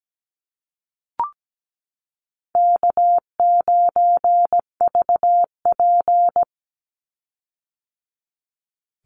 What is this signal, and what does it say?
Morse code, heard as a keyed tone, about 17 wpm, 700 Hz. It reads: K9VP
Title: Easy Call Signs - 17wpm